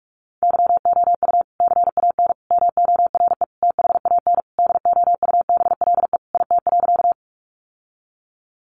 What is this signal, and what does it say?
Morse code, heard as a keyed tone, about 44 wpm, 700 Hz. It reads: YOU CAN MORE THAN DOUBLE IT.
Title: Sentences from Top 1400 Words 45wpm